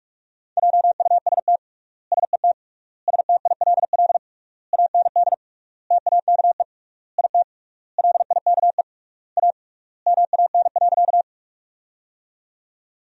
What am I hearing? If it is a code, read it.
Morse code, heard as a keyed tone, about 47 wpm, 700 Hz. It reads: JUST SET STILL AND TAKE IT LIKE A MAN.
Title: 3x Word-Spacing - Sentences from Top 300 Words - 45wpm